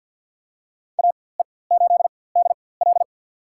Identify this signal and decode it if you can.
Morse code, heard as a keyed tone, about 47 wpm, spaced as for 23 wpm, 700 Hz. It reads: AE8DL
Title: USA Call Signs - 5 Characters - 4x Character Spacing - 50wpm